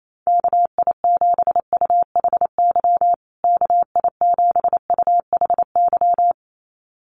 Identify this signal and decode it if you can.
Morse code, heard as a keyed tone, about 28 wpm, 700 Hz. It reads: KI7UHY KI7UHY
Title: USA Call Signs Repeated 2x 28wpm